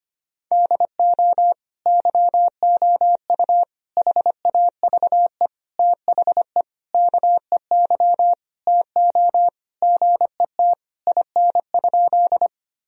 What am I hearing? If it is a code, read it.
Morse code, heard as a keyed tone, about 25 wpm, 700 Hz. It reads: DO YOU HAVE THE KEY TO GET IN?
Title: Sentences from Top 1000 Words - Review Part 4 - 25wpm